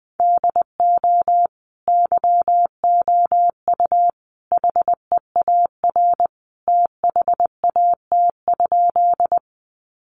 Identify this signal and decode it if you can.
Morse code, heard as a keyed tone, about 20 wpm, 700 Hz. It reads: DO YOU HEAR THAT?
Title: Sentences from Top 300 Words 20wpm